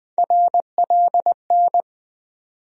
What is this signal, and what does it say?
Morse code, heard as a keyed tone, about 20 wpm, 700 Hz. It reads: RLN
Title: USA Call Sign Postfixes - 3 Letters 20wpm